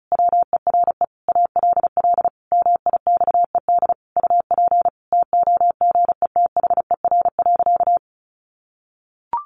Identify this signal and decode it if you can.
Morse code, heard as a keyed tone, about 34 wpm, 700 Hz. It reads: WERE ALL MIXED UP TOGETHER.
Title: Sentences from Top 1500 Words 35wpm